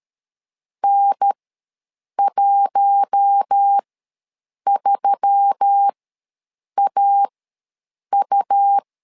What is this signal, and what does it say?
Morse code, heard as a keyed tone, about 13 wpm, spaced as for 7 wpm, 800 Hz. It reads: N13AU
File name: LEZIONI / 4 Q1S3 ws 12wpm 2038